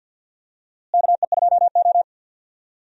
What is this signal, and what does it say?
Morse code, heard as a keyed tone, about 51 wpm, 700 Hz. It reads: KE2O
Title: USA Call Signs - 4 Characters - Repeated 2x - 50wpm